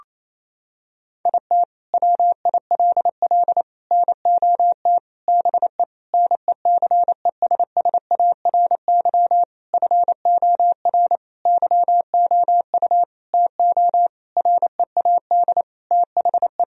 Morse code, heard as a keyed tone, about 28 wpm, 700 Hz. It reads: IT WILL NOT BE NECESSARY FOR YOU TO READ THE
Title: Sentences from Top 1000 Words - Review Part 1 - 28wpm